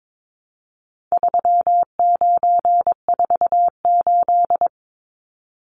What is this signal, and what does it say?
Morse code, heard as a keyed tone, about 22 wpm, 700 Hz. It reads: 3948
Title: Numbers - 4-Digit 22wpm